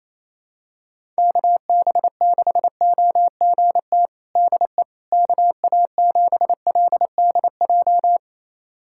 Morse code, heard as a keyed tone, about 28 wpm, 700 Hz. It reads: KB6OGT DE KA7LDJ
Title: USA Callsign DE Callsign - 6 Characters - 28wpm